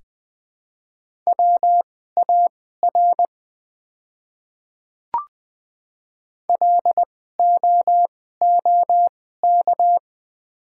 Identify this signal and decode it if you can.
Morse code, heard as a keyed tone, about 20 wpm, spaced as for 15 wpm, 700 Hz. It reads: WAR LOOK
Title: 2x Character Spacing - Top 500 Words - 20wpm